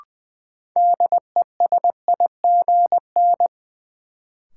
Morse code, heard as a keyed tone, about 20 wpm, 700 Hz. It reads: DESIGN
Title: Sentences from Top 800 Words - New Word Drill - 20wpm